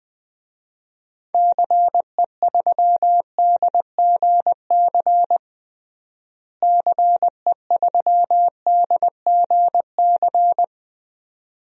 Morse code, heard as a keyed tone, about 20 wpm, 700 Hz. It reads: CE3DGC CE3DGC
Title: DXCC-100 Call Signs - 6 Characters - Repeated 2x - 20wpm